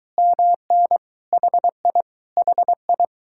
Morse code, heard as a keyed tone, about 22 wpm, 700 Hz. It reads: MN HI HI